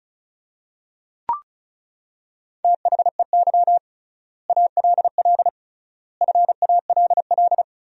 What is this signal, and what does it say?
Morse code, heard as a keyed tone, about 34 wpm, 700 Hz. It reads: THEY ALL FALL
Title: 3x Word-Spacing - Sentences from Top 500 Words - 35wpm